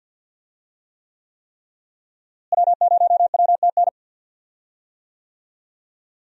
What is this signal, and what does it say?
Morse code, heard as a keyed tone, about 48 wpm, 700 Hz. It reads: W0WTN
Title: USA Call Signs - 5 Characters - Repeated 3x - 50wpm